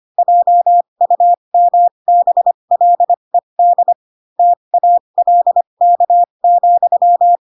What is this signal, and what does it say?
Morse code, heard as a keyed tone, about 25 wpm, spaced as for 22 wpm, 700 Hz. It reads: JUMBLED TALK,
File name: Pinocchio / Pinocchio ch7 18 25 ik1hge